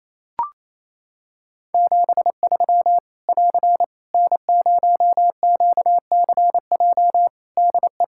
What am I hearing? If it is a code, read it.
Morse code, heard as a keyed tone, about 28 wpm, 700 Hz. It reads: 73 <AR> N0QCJ DE
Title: K1USN SST QSO Practice 28wpm